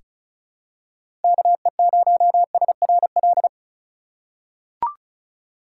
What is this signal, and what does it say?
Morse code, heard as a keyed tone, about 36 wpm, 700 Hz. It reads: KE0SRL
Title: K1USN SST QSO Practice - 35wpm